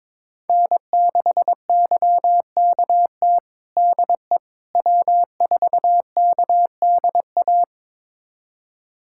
Morse code, heard as a keyed tone, about 22 wpm, 700 Hz. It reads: N6YKT DE W4KDA
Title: USA Callsign DE Callsign - 5 Characters - 22wpm